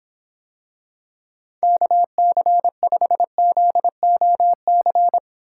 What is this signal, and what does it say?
Morse code, heard as a keyed tone, about 26 wpm, 700 Hz. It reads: KC5ZOC